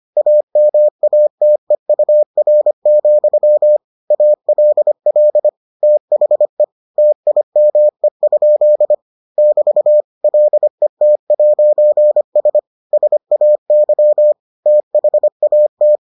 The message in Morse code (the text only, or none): AMATEUR, ALL THE TIME? <BT> LET'S SAY THAT